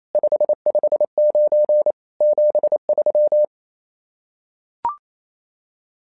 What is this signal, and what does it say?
Morse code, heard as a keyed tone, about 28 wpm, 600 Hz. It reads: 559 73
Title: SOTA QSO Practice 28wpm